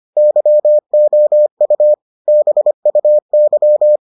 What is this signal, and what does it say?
Morse code, heard as a keyed tone, about 25 wpm, 600 Hz. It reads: YOU BUY